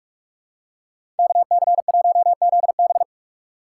Morse code, heard as a keyed tone, about 43 wpm, 700 Hz. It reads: KC1ZB
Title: USA Call Signs - 5 Characters - 3x Character Spacing - 45wpm